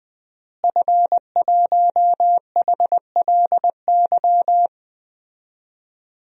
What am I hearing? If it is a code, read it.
Morse code, heard as a keyed tone, about 21 wpm, 700 Hz. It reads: F1HLY
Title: DXCC-100 Call Signs - 5 Characters - Repeated 3x - 20wpm